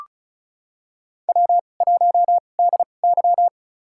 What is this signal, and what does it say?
Morse code, heard as a keyed tone, about 34 wpm, spaced as for 25 wpm, 700 Hz. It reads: W1DY
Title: USA Call Signs - 4 Characters - 2x Character Spacing - 35wpm